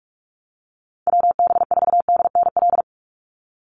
Morse code, heard as a keyed tone, about 44 wpm, 700 Hz. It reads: WB4DNL
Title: Call Signs Repeated 2x - 45wpm